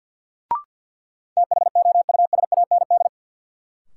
Morse code, heard as a keyed tone, about 52 wpm, 700 Hz. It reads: THOUSAND
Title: Sentences from Top 600 Words - New Word Drill - 50wpm